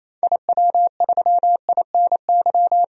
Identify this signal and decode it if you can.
Morse code, heard as a keyed tone, about 28 wpm, 700 Hz. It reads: IW3INY
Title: DXCC-100 Call Signs - 6 Characters - 28wpm